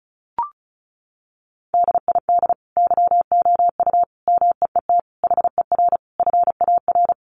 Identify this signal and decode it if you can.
Morse code, heard as a keyed tone, about 34 wpm, 700 Hz. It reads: DID YOU MEET HER FAR
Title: Sentences from Top 1100 Words 35wpm